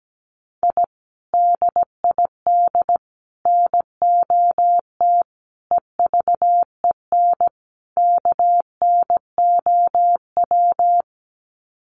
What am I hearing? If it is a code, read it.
Morse code, heard as a keyed tone, about 17 wpm, 700 Hz. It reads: I DID NOT EVEN KNOW
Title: Sets of 5 Words Encore 17wpm